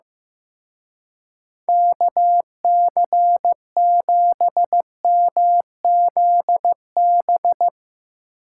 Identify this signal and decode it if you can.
Morse code, heard as a keyed tone, about 16 wpm, 700 Hz. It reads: KC7MZB
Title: Speed Racing - USA 2x3 Call Signs 15wpm